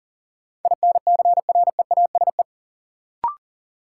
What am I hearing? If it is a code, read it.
Morse code, heard as a keyed tone, about 40 wpm, 700 Hz. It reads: INCREASE
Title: Sentences from Top 1200 Words - New Word Drill Alt - 40wpm